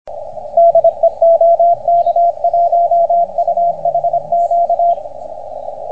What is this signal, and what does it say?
Morse code, heard as a keyed tone, about 26 wpm, 690 Hz, with radio noise. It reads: DE OK1UHQ